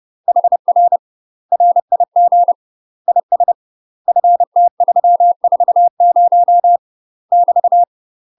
Morse code, heard as a keyed tone, about 30 wpm, 700 Hz. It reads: HR RIG IS FT340 <BT>
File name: UK MASTER QSOs TRAINING / 30WPM / UK MASTER QSOs 0011